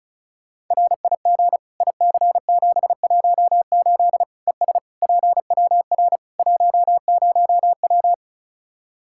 Morse code, heard as a keyed tone, about 36 wpm, 700 Hz. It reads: RIG IC718 ES PWR 10W
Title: Practice QSO - Exchange 2 - Info - 35wpm